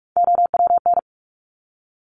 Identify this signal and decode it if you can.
Morse code, heard as a keyed tone, about 46 wpm, 700 Hz. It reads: OWN
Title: Top Three Letter Combinations in Words 45wpm